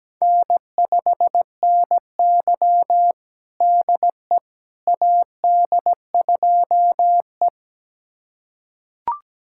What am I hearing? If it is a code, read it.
Morse code, heard as a keyed tone, about 17 wpm, 700 Hz. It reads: N5NY DE AD2E
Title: USA Callsign DE Callsign - 4 Characters - 17wpm